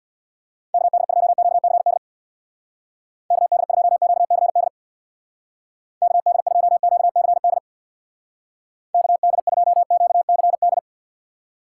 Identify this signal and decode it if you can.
Morse code, heard as a keyed tone, about 64 wpm, 700 Hz. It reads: KD2QCD KD2QCD KD2QCD KD2QCD